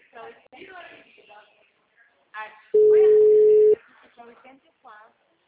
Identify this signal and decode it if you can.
Ringback tone (Europe, 425 Hz), with speech.